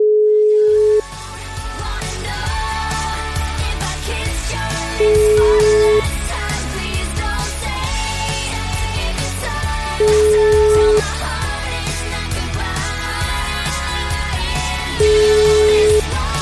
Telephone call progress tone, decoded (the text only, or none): ringback tone